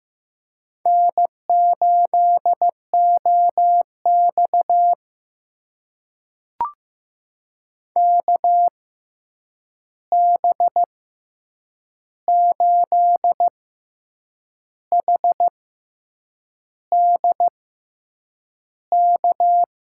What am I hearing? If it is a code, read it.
Morse code, heard as a keyed tone, about 15 wpm, spaced as for 7 wpm, 700 Hz. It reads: N8OX KB8HDK
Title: USA Call Signs - 6x Character Spacing - 15wpm